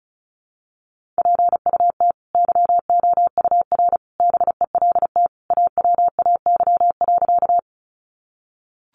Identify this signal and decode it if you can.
Morse code, heard as a keyed tone, about 34 wpm, 700 Hz. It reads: PUT YOUR BELT AWAY.
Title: Sentences from Top 1400 Words 35wpm